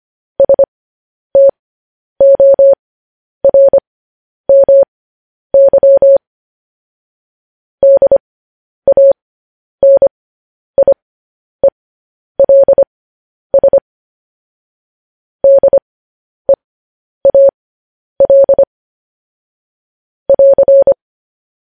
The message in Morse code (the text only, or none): STORMY DANIELS DEAL <AR>